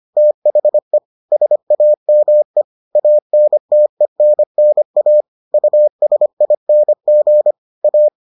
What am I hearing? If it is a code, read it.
Morse code, heard as a keyed tone, about 25 wpm, 600 Hz. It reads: THE SAME ANTENNA USING A